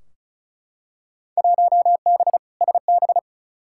Morse code, heard as a keyed tone, about 34 wpm, 700 Hz. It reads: 1B SB
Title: Field Day Exchanges - 35wpm